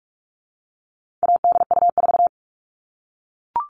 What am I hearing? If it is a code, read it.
Morse code, heard as a keyed tone, about 44 wpm, 700 Hz. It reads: ADU4
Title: All Cut Numbers - 4-Digit 45wpm